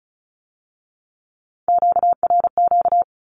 Morse code, heard as a keyed tone, about 36 wpm, 700 Hz. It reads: QRQ
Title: Q-Codes with Spoken Meaning - 35wpm